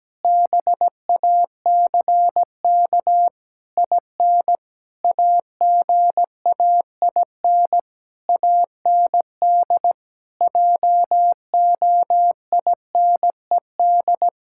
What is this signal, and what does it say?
Morse code, heard as a keyed tone, about 17 wpm, 700 Hz. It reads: BACK IN AGAIN AND JOINED
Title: Pangram Challenge v2 - 17wpm